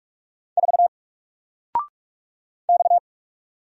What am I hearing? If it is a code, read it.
Morse code, heard as a keyed tone, about 44 wpm, 700 Hz. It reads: 4 X
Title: Single Letter-Number 45wpm